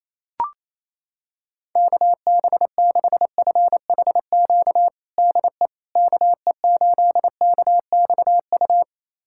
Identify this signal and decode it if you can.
Morse code, heard as a keyed tone, about 28 wpm, 700 Hz. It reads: KB6FHQ DE KE8KXU
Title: USA Callsign DE Callsign 28wpm